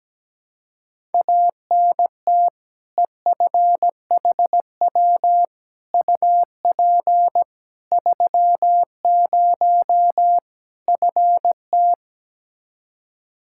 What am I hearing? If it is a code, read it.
Morse code, heard as a keyed tone, about 17 wpm, 700 Hz. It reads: ANT EFHW UP 30 FT
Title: Practice QSO - Exchange 2 - Info - 17wpm